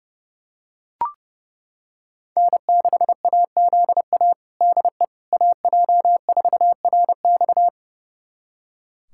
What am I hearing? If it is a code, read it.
Morse code, heard as a keyed tone, about 32 wpm, 700 Hz. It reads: N6AZA DE AJ4RX
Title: USA Callsign DE Callsign - 5 Characters - 30wpm